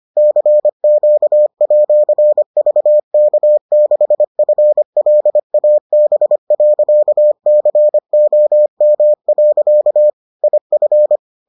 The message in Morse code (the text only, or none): CQ@VK6FLAB.COM. IF